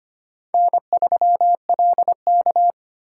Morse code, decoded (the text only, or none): N3LK